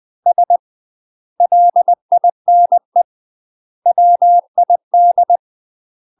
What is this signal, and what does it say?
Morse code, heard as a keyed tone, about 20 wpm, 700 Hz. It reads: S LINE WID
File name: UK MASTER QSOs TRAINING / 20WPM / UK MASTER QSOs 0001